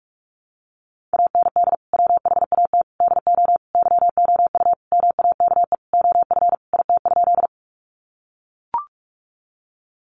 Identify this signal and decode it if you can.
Morse code, heard as a keyed tone, about 44 wpm, 700 Hz. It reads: AND WHAT DO YOU MAKE OF IT?
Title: Sentences from Top 100 Words 45wpm